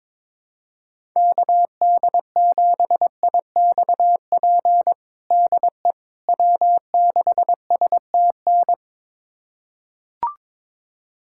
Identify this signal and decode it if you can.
Morse code, heard as a keyed tone, about 22 wpm, 700 Hz. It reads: KD7IXP DE W6STN
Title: USA Callsign DE Callsign 22wpm